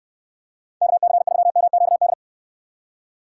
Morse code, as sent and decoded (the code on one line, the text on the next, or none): -.- -.-. ...-- -- --.- -..
KC3MQD